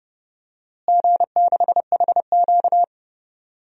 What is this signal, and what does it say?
Morse code, heard as a keyed tone, about 30 wpm, 700 Hz. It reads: G6HQ
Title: DXCC-100 Call Signs - 4 Characters - Repeated 2x - 30wpm